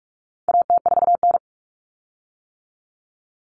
Morse code, heard as a keyed tone, about 44 wpm, 700 Hz. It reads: AT4N
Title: Common Cut Numbers - 4-Digit 45wpm